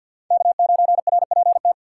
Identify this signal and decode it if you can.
Morse code, heard as a keyed tone, about 53 wpm, 700 Hz. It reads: K9RPT
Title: USA Call Signs - 3x Character Spacing - 50wpm